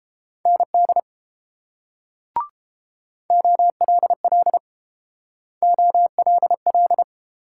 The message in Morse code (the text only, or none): ND OLL OLL